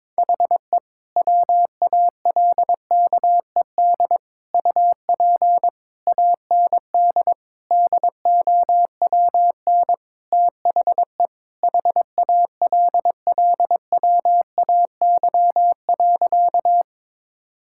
Morse code, heard as a keyed tone, about 21 wpm, 700 Hz. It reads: HE WALKED UP AND DOWN THE HALLWAY.
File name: practice / 22wpm / Sentences with 4-Word Phrases - 22wpm